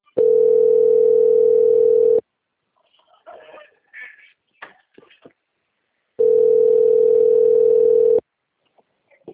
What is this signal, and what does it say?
Ringback tone (North America), 440 + 480 Hz.
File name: out- - - - - .18924